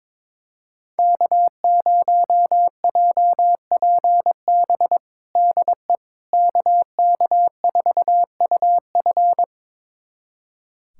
Morse code, heard as a keyed tone, about 22 wpm, 700 Hz. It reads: K0JPB DE KK4UF
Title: USA Callsign DE Callsign - 5 Characters - 22wpm